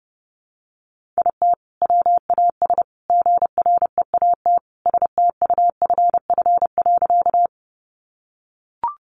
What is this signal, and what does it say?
Morse code, heard as a keyed tone, about 30 wpm, 700 Hz. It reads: IT WAS GREAT STUFF.
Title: Sentences from Top 1400 Words 30wpm